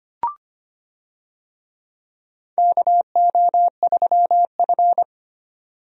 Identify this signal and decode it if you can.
Morse code, heard as a keyed tone, about 25 wpm, 700 Hz. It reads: KO3F
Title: USA Call Signs - 4 Characters - 25wpm